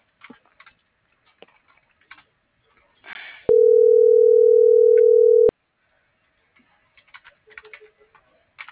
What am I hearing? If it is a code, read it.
Ringback tone (North America), 440 + 480 Hz.